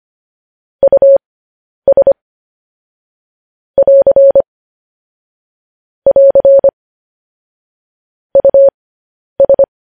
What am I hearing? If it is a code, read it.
Morse code, heard as a keyed tone, about 25 wpm, spaced as for 10 wpm, 550 Hz. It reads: US <AR> <AR> US
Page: news headlines in morse code at 10 wpm